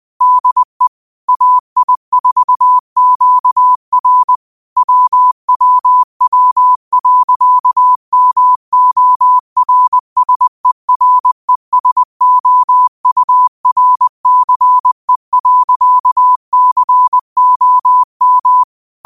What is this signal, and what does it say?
Morse code, heard as a keyed tone, about 20 wpm, 1000 Hz. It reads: DE AI4QR WWW.MORSERESOURCE.COM
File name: qotdfiles / 20wpm